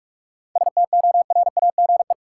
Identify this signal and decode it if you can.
Morse code, heard as a keyed tone, about 46 wpm, 700 Hz. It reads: STORAGE
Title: Seven Letter Words - 45wpm